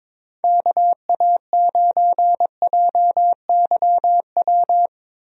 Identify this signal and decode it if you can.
Morse code, heard as a keyed tone, about 21 wpm, 700 Hz. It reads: KA9JYW
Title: USA Call Signs - 6 Characters - 22wpm